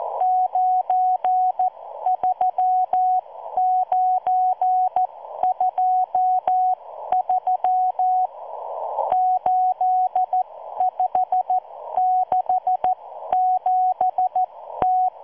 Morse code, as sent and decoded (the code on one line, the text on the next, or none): ----. ...-- ----. ..--- ...-- / ---.. ..... -.... --... -
93923 8567T